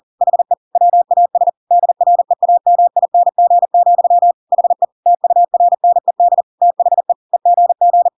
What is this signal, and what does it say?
Morse code, heard as a keyed tone, about 40 wpm, 700 Hz. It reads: HE WAS DREAMING, HE TURNED THE EGG
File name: Pinocchio / Pinocchio ch5 40 40 ik1hge